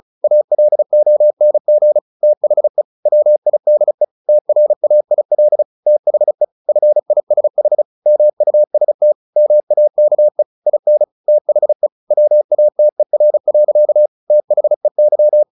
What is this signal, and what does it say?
Morse code, heard as a keyed tone, about 35 wpm, 600 Hz. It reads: ALONG THE WIDE TRAIL THE FISH MUST MAKE IN THE WATER. THEY
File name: the old man and the sea 2 35wpm